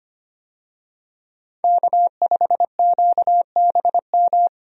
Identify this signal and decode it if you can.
Morse code, heard as a keyed tone, about 25 wpm, 700 Hz. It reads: K5QBM
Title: USA Call Signs - 5 Characters - 25wpm